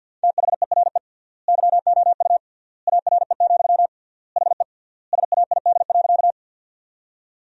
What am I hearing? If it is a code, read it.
Morse code, heard as a keyed tone, about 52 wpm, 700 Hz. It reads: THERE YOU ARE, HE SAID.
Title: 3x Word-Spacing - Sentences from Top 100 Words - 50wpm